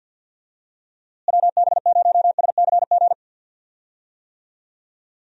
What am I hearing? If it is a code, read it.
Morse code, heard as a keyed tone, about 46 wpm, 700 Hz. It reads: WB0SCG
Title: USA Call Signs - 6 Characters - Repeated 2x - 50wpm